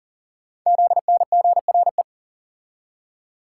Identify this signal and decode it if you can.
Morse code, heard as a keyed tone, about 40 wpm, 700 Hz. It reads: ZNGRE
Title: Code Groups - 5 Letters - 40wpm